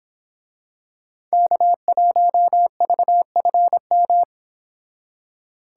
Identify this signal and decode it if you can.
Morse code, heard as a keyed tone, about 27 wpm, 700 Hz. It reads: K1VFM
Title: Speed Racing - USA 1x3 Call Signs 22wpm